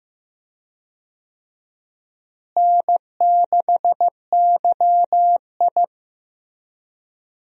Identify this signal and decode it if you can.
Morse code, heard as a keyed tone, about 15 wpm, 700 Hz. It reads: N6YI
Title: USA Call Signs - 4 Characters - Repeated 2x - 15wpm